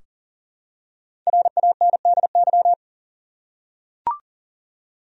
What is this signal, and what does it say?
Morse code, heard as a keyed tone, about 40 wpm, 700 Hz. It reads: RANDY
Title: USA Common Names - Max 5 Letters - 40wpm